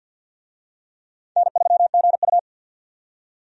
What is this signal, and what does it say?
Morse code, heard as a keyed tone, about 48 wpm, 700 Hz. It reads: N3GU
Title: USA Call Signs - 4 Characters - 50wpm